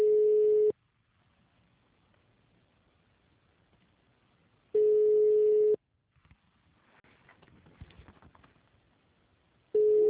Ringback tone (Europe, 425 Hz).